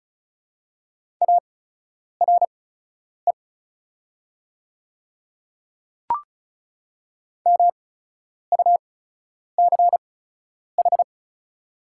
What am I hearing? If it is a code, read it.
Morse code, heard as a keyed tone, about 34 wpm, spaced as for 10 wpm, 690 Hz. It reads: ARE MUCH
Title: 8x Character Spacing - Top 200 Words - 35wpm